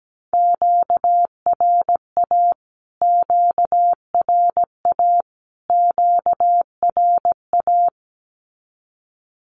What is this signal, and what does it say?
Morse code, heard as a keyed tone, about 17 wpm, 700 Hz. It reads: QRA QRA QRA
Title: Q-Codes Repeated 3x 17wpm